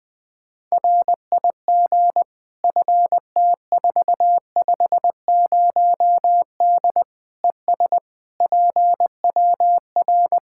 Morse code, heard as a keyed tone, about 20 wpm, 700 Hz. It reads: RIG FT450D ES PWR
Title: K1USN SST QSO Practice 20wpm